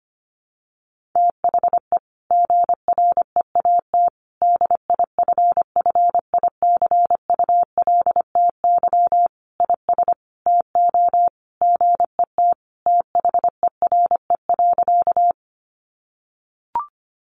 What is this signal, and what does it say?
Morse code, heard as a keyed tone, about 25 wpm, 700 Hz. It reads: THE GREAT DIFFICULTY IS TO GET THERE.
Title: Sentences from Top 1500 Words 25wpm